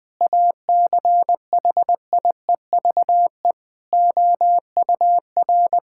Morse code, heard as a keyed tone, about 20 wpm, 700 Hz. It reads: ACHIEVE OUR
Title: Sentences from Top 1000 Words - Review Part 6 - 20wpm